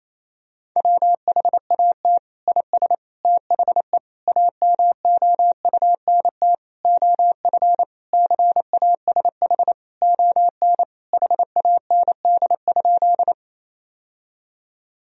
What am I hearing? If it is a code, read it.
Morse code, heard as a keyed tone, about 28 wpm, 700 Hz. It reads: WHAT IS THE AMOUNT OF CASH ON HAND?
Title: Sentences from Top 1000 Words - Review Part 4 - 28wpm